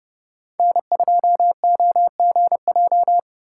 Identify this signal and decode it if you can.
Morse code, heard as a keyed tone, about 30 wpm, 700 Hz. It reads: N2OGJ
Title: USA Call Signs - 5 Characters - 3x Character Spacing - 30wpm